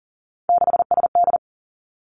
Morse code, heard as a keyed tone, about 39 wpm, 690 Hz. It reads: 6SD
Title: Three Challenging Letter-Number Combinations 40wpm